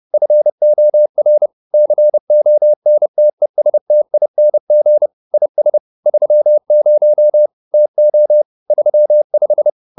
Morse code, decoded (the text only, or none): FOR CONTESTING IS 30 TO 35